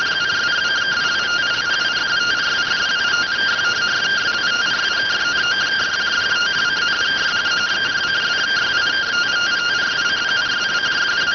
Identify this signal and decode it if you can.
Two-tone teleprinter signal, about 170 Hz shift.